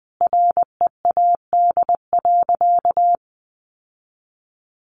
Morse code, heard as a keyed tone, about 20 wpm, 700 Hz. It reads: READ.
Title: Sentences from Top 1400 Words 20wpm